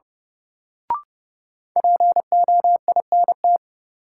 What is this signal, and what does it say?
Morse code, heard as a keyed tone, about 30 wpm, 700 Hz. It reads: POINT
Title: Sentences from Top 200 Words - New Word Drill - 30wpm